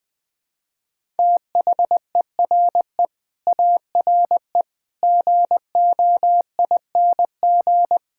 Morse code, heard as a keyed tone, about 21 wpm, 700 Hz. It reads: THERE ARE GOING
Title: Sets of 8 Words - 20wpm